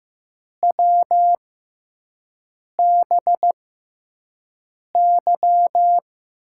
Morse code, heard as a keyed tone, about 15 wpm, spaced as for 5 wpm, 700 Hz. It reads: WBY
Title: USA Call Signs - 5 Characters - 6x Character Spacing - 15wpm